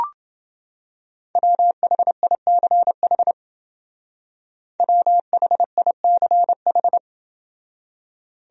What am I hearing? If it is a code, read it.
Morse code, heard as a keyed tone, about 28 wpm, 700 Hz. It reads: WHICH WHICH